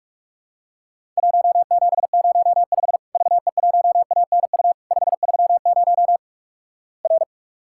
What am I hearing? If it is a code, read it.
Morse code, heard as a keyed tone, about 46 wpm, 700 Hz. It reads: 1705 VE1ANU 530 R
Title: Worked All Europe DX Contest - QTC Exchange - 45wpm